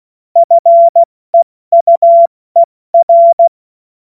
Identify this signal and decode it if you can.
Morse code, heard as a keyed tone, about 16 wpm, 680 Hz. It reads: FEUER
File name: Morsekurs / Witze / Witz 02. Mai 2023 70 F80